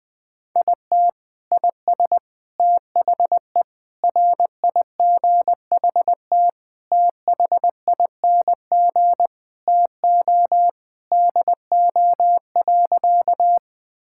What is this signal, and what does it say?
Morse code, heard as a keyed tone, about 20 wpm, 700 Hz. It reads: IT IS THE RIGHT THING TO DO.
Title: Sentences with 5-Word Phrases - 20wpm